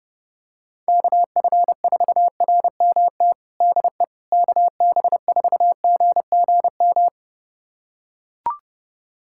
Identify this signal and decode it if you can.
Morse code, heard as a keyed tone, about 30 wpm, 700 Hz. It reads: KF4RMT DE KB4GGM
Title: USA Callsign DE Callsign - 6 Characters - 30wpm